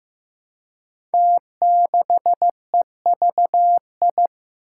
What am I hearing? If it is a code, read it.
Morse code, heard as a keyed tone, about 15 wpm, 700 Hz. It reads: T6EVI
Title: Five Challenging Letter-Number Combinations 15wpm